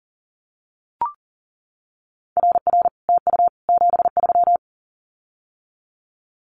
Morse code, heard as a keyed tone, about 38 wpm, 700 Hz. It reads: RR TU 73
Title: Common QSO Phrases Encore 40wpm